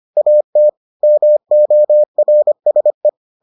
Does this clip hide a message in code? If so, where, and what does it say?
Yes: Morse code from 0.2 to 3.1 s, heard as a keyed tone, about 25 wpm, 600 Hz. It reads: AT MORSE